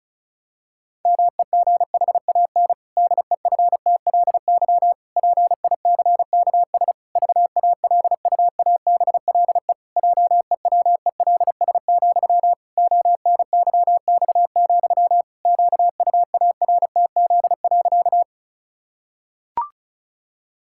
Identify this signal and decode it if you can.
Morse code, heard as a keyed tone, about 34 wpm, 700 Hz. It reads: MEGHAN DEFTLY PICKS VALUABLE JEWELS, ONYX, QUARTZ.
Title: Pangram Challenge v2 - 35wpm